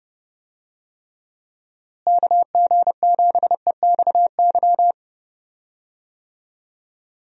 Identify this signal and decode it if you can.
Morse code, heard as a keyed tone, about 30 wpm, 700 Hz. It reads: KG7EXY
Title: USA Call Signs - 6 Characters - Repeated 2x - 30wpm